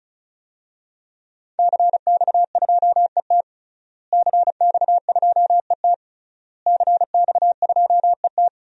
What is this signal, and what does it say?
Morse code, heard as a keyed tone, about 36 wpm, 700 Hz. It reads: CX2ET CX2ET CX2ET
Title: DXCC-100 Call Signs - 5 Characters - Repeated 3x - 35wpm